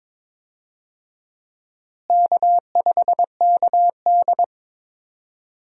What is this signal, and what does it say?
Morse code, heard as a keyed tone, about 23 wpm, 700 Hz. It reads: K5KD
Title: USA Call Signs - 4 Characters - Repeated 2x - 22wpm